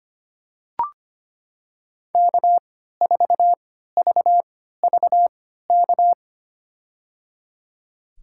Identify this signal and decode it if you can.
Morse code, heard as a keyed tone, about 25 wpm, spaced as for 14 wpm, 700 Hz. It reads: K4VVK
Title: USA Call Signs - 5 Characters - 3x Character Spacing - 25wpm